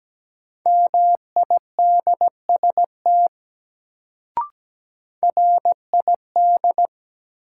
Morse code, heard as a keyed tone, about 17 wpm, 700 Hz. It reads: MIDST RID
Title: Sets of 4 Words - New Word Drill - 17wpm